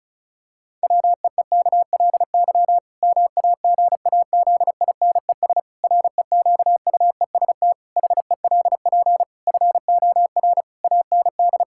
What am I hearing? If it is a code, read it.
Morse code, heard as a keyed tone, about 36 wpm, 700 Hz. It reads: WEEKLY MAGAZINES REQUEST HELP FOR AND
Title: Pangram Challenge v2 - 35wpm